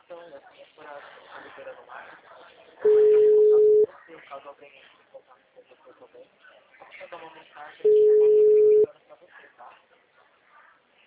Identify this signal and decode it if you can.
Ringback tone (Europe, 425 Hz), with speech.